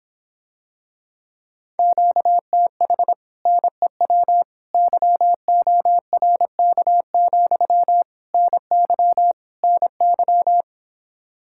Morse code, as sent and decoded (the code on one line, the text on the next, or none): --.- - .... / -. .-- / -.-- --- .-. -.- --..-- / -. -.-- / -. -.--
QTH NEW YORK, NY NY